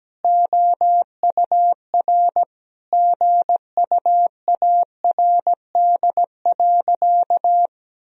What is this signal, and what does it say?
Morse code, heard as a keyed tone, about 17 wpm, 700 Hz. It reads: OUR GUARD.
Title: Sentences from Top 1000 Words - Review Part 4 - 17wpm